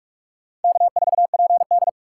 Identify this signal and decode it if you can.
Morse code, heard as a keyed tone, about 46 wpm, 700 Hz. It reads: K4PD
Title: USA Call Signs - 4 Characters - 4x Character Spacing - 45wpm